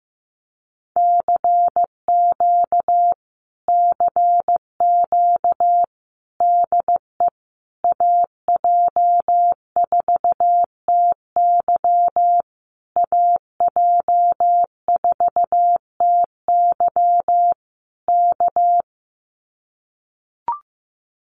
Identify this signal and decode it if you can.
Morse code, heard as a keyed tone, about 15 wpm, 700 Hz. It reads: CQ CQ DE AJ4TY AJ4TY K
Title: Common QSO Phrases Encore 15wpm